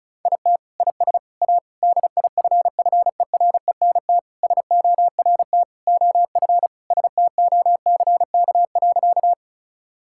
Morse code, heard as a keyed tone, about 36 wpm, 700 Hz. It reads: IT IS A DIFFERENT SORT OF STOCK.
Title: Sentences from Top 1000 Words - Review Part 5 - 35wpm